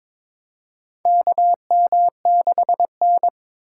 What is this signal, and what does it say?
Morse code, heard as a keyed tone, about 23 wpm, 700 Hz. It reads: KM6N
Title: USA Call Signs - 4 Characters - 4x Character Spacing - 22wpm